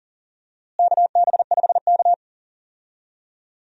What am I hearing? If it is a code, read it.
Morse code, heard as a keyed tone, about 40 wpm, 700 Hz. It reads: KB5K
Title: USA Call Signs - 4 Characters - Repeated 3x - 40wpm